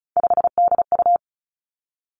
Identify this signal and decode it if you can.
Morse code, heard as a keyed tone, about 36 wpm, 700 Hz. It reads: 5DU
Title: Three Challenging Letter-Number Combinations 35wpm